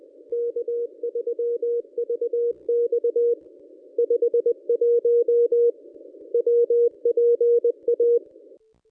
Morse code, heard as a keyed tone, about 26 wpm, 440 Hz. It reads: K3VX 51 WPA
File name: contests / ss-cw-k5tr-op-n5rz -1100